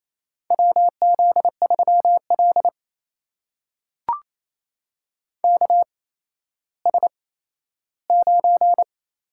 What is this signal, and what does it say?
Morse code, heard as a keyed tone, about 28 wpm, spaced as for 14 wpm, 700 Hz. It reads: WZ3L KS9
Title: USA Call Signs - 4 Characters - 8x Character Spacing - 28wpm